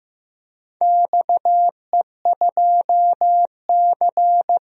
Morse code, heard as a keyed tone, about 15 wpm, 700 Hz. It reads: XE2C
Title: DXCC-100 Call Signs - 4 Characters - Repeated 3x - 15wpm